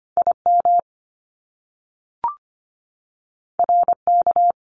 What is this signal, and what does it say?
Morse code, heard as a keyed tone, about 25 wpm, 700 Hz. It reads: IM RK
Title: Top Two Letter Combinations in Words 25wpm